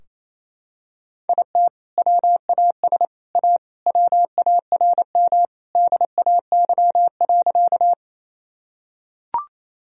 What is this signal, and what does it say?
Morse code, heard as a keyed tone, about 28 wpm, 700 Hz. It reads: IT WAS A WARM DAY.
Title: Sentences from Top 1000 Words - Review Part 2 - 28wpm